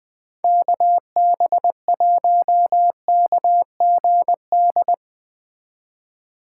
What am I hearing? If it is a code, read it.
Morse code, heard as a keyed tone, about 20 wpm, 700 Hz. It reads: KB1KGD